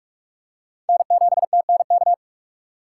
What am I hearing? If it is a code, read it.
Morse code, heard as a keyed tone, about 46 wpm, 700 Hz. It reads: N7TNK
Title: USA Call Signs - 5 Characters - 3x Character Spacing - 45wpm